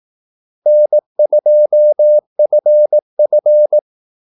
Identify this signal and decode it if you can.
Morse code, heard as a keyed tone, about 18 wpm, 600 Hz. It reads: N2FF
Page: Kaldesignaler 18 wpm